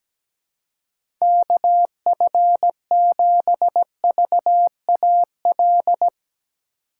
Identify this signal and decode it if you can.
Morse code, heard as a keyed tone, about 17 wpm, 700 Hz. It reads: KF7VAL
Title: USA Call Signs - 6 Characters - 4x Character Spacing - 17wpm